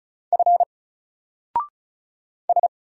Morse code, heard as a keyed tone, about 36 wpm, 700 Hz. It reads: F S
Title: Single Letter-Number Alt 35wpm